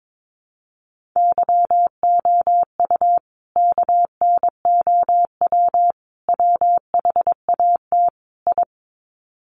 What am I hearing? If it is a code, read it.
Morse code, heard as a keyed tone, about 22 wpm, 700 Hz. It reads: YOU KNOW WHAT I
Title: Sets of 4 Words 22wpm